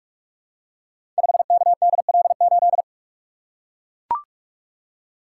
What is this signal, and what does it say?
Morse code, heard as a keyed tone, about 44 wpm, 700 Hz. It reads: 5KDL8
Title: Code Groups - 5 Letter:Numbers - 45wpm